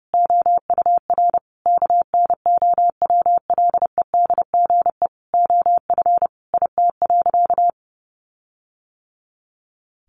Morse code, heard as a keyed tone, about 30 wpm, 700 Hz. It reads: OUR KNOWLEDGE OF IT.
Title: Sentences from Top 1400 Words 30wpm